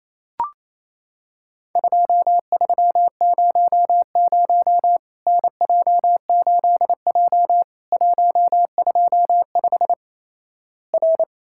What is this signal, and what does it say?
Morse code, heard as a keyed tone, about 28 wpm, 700 Hz. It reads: 2300 NJ8J 125 R
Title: Worked All Europe DX Contest - QTC Exchange - 28wpm